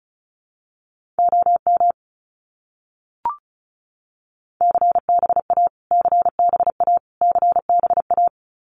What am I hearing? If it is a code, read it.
Morse code, heard as a keyed tone, about 34 wpm, 700 Hz. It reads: OM CBA CBA CBA
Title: Abbreviations Repeated 3x - 35wpm